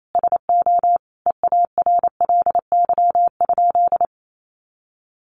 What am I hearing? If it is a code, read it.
Morse code, heard as a keyed tone, about 28 wpm, 700 Hz. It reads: SO EARLY?
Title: Sentences from Top 400 Words 28wpm